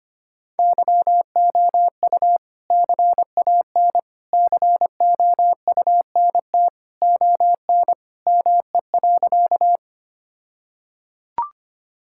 Morse code, heard as a keyed tone, about 25 wpm, 700 Hz. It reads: YOU CAN COUNT ON ME.
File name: practice / 25wpm / All-Age Appropriate Jokes - 25wpm